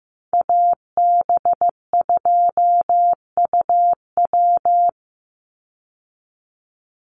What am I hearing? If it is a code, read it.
Morse code, heard as a keyed tone, about 15 wpm, 700 Hz. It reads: AB2UW
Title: USA Call Signs 15wpm